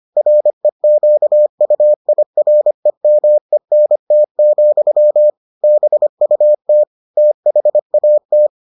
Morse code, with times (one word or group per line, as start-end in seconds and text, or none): 0.2-5.3 REQUIREMENT,
5.6-6.8 BUT
7.2-8.5 THAT